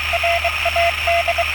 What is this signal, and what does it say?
Morse code, heard as a keyed tone, about 22 wpm, 680 Hz, with radio noise. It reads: RAD